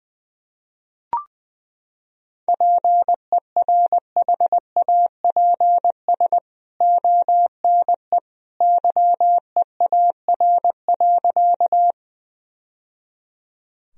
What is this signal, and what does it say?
Morse code, heard as a keyed tone, about 21 wpm, 700 Hz. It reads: PERHAPS ONE YEAR.
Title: Sentences from Top 1000 Words - Review Part 1 - 20wpm